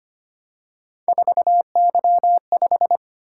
Morse code, heard as a keyed tone, about 25 wpm, 700 Hz. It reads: 4Y5
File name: Three Letter-Number Combinations - 25wpm